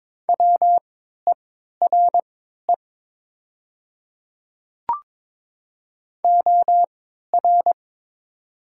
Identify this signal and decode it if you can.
Morse code, heard as a keyed tone, about 23 wpm, spaced as for 13 wpm, 700 Hz. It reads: WERE OR
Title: 3x Character Spacing - Top 100 Words - 22wpm